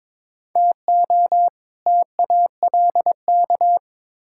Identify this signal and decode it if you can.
Morse code, heard as a keyed tone, about 22 wpm, 700 Hz. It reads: TO TALK
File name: practice / 22wpm / Build A Phrase - 5-Word Phrases - 22wpm